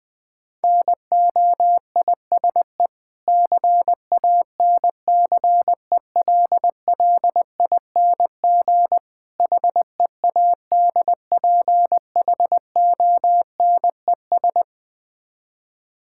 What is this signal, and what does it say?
Morse code, heard as a keyed tone, about 20 wpm, 700 Hz. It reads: NOISE CANCELLING HEADPHONES
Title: Difficult Words with Hints - 20wpm